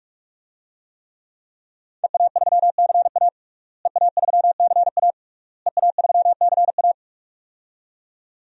Morse code, heard as a keyed tone, about 46 wpm, 700 Hz. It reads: EA3CA EA3CA EA3CA
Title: DXCC-100 Call Signs - 5 Characters - Repeated 3x - 45wpm